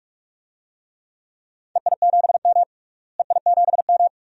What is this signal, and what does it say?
Morse code, heard as a keyed tone, about 46 wpm, 700 Hz. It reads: EI7M EI7M
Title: DXCC-100 Call Signs - 4 Characters - Repeated 2x - 45wpm